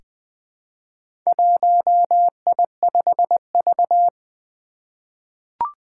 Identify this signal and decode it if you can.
Morse code, heard as a keyed tone, about 20 wpm, 700 Hz. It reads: 1I5V
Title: Four Challenging Letter-Number Combinations 20wpm